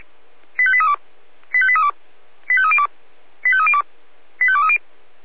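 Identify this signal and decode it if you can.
Five-tone selective call (ZVEI), heard as quick runs of five tones, tones of about 70 ms. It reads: 85832 85832 85302 85302 85230
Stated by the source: ZVEI five-tone selective call 85201, received on SDR.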